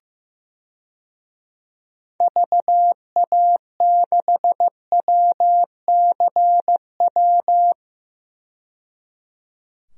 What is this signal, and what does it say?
Morse code, heard as a keyed tone, about 15 wpm, 700 Hz. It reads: VA6WCW
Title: DXCC-100 Call Signs - 6 Characters - 15wpm